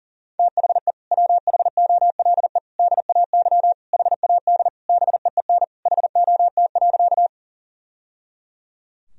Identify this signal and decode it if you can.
Morse code, heard as a keyed tone, about 37 wpm, 700 Hz. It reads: THE WHOLE DAY HAD BEEN HOT.
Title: Sentences from Top 1000 Words - Review Part 3 - 40wpm